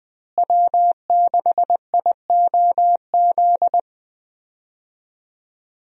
Morse code, heard as a keyed tone, about 20 wpm, 700 Hz. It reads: W6IOZ
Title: USA Call Signs - 5 Characters - Repeated 2x - 20wpm